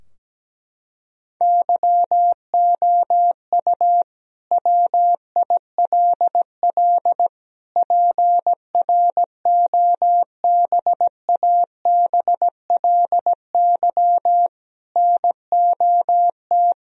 Morse code, heard as a keyed tone, about 17 wpm, 700 Hz. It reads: YOU WILL PROBABLY NOT
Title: Sentences from Top 1000 Words - Review Part 4 - 17wpm